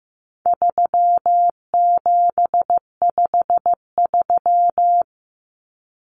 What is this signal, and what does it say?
Morse code, heard as a keyed tone, about 15 wpm, 700 Hz. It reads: 3753
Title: Common Cut Numbers - 4-Digit 15wpm